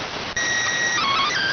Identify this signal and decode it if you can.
SSTV (slow-scan television), PD 290.